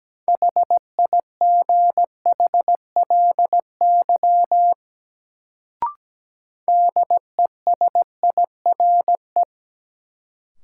Morse code, heard as a keyed tone, about 17 wpm, 700 Hz. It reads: HIGHLY DESIRE
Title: Sentences from Top 1100 Words - New Word Drill - 17wpm